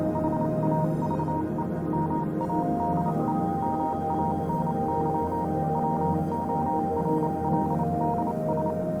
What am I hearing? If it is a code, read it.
Morse code, heard as a keyed tone, about 28 wpm, spaced as for 25 wpm, 940 Hz, with noise. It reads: HAVE MADE MANY WORLDS
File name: Vylet Pony / aura / do you remember the song she sang (morse code version)